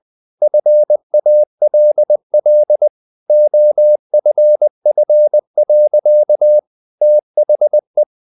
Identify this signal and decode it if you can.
Morse code, heard as a keyed tone, about 20 wpm, 600 Hz. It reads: FALL OFF. THE
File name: the old man and the sea part 2 20wpm